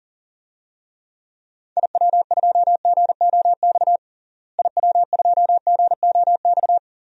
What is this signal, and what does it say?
Morse code, heard as a keyed tone, about 40 wpm, 700 Hz. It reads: IW2GOX IW2GOX
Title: DXCC-100 Call Signs - 6 Characters - Repeated 2x - 40wpm